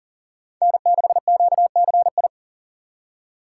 Morse code, heard as a keyed tone, about 40 wpm, 700 Hz. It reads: N6QCI
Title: USA Call Signs - 4x Character Spacing - 40wpm